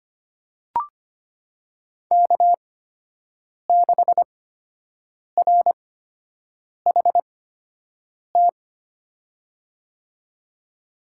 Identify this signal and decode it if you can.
Morse code, heard as a keyed tone, about 25 wpm, spaced as for 7 wpm, 700 Hz. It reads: K6RHT
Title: USA Call Signs - 5 Characters - 8x Character Spacing - 25wpm